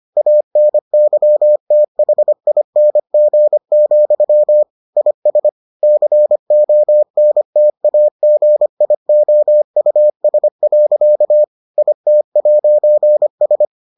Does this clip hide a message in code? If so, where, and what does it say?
Yes: Morse code from 0.2 to 13.7 s, heard as a keyed tone, about 25 wpm, 600 Hz. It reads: ANYTHING, IS CONTAGIOUS. IT'S